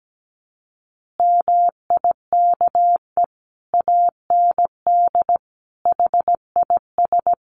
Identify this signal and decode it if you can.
Morse code, heard as a keyed tone, about 17 wpm, 700 Hz. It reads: MIKE AND HIS
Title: Additional Easy Phrases 17wpm